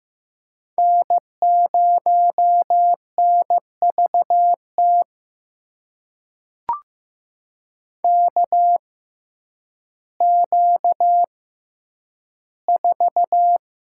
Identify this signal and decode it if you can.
Morse code, heard as a keyed tone, about 15 wpm, spaced as for 9 wpm, 700 Hz. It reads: N0NVT KQ4
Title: USA Call Signs - 5 Characters - 6x Character Spacing - 15wpm